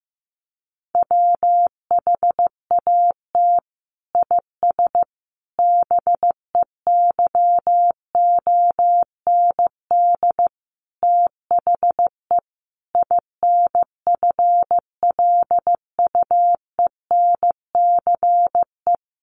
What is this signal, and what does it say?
Morse code, heard as a keyed tone, about 15 wpm, 700 Hz. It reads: WHAT IS BEYOND THE INFLUENCE
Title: Sentences from Top 1100 Words 15wpm